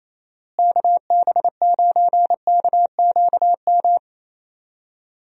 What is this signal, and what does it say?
Morse code, heard as a keyed tone, about 28 wpm, 700 Hz. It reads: KB9KQM
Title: USA Call Signs - 6 Characters - Repeated 2x - 28wpm